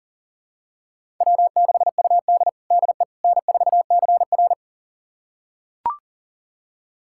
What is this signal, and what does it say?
Morse code, heard as a keyed tone, about 39 wpm, 700 Hz. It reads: W6UD DE N4CR
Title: USA Callsign DE Callsign - 4 Characters - 40wpm